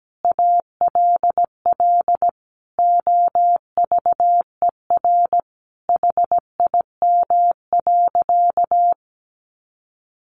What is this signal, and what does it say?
Morse code, heard as a keyed tone, about 17 wpm, 700 Hz. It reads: ALL OVER HIM.
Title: Sentences from Top 1100 Words 17wpm